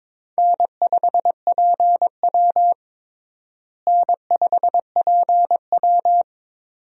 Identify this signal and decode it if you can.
Morse code, heard as a keyed tone, about 22 wpm, 700 Hz. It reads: N5PW N5PW
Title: USA Call Signs - 4 Characters - Repeated 2x - 22wpm